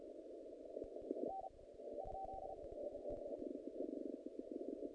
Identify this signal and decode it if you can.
Morse code, heard as a keyed tone, about 36 wpm, 760 Hz, with noise. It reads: N ?
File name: contests / naqp-cw-k5tr-op-k5pi- -2300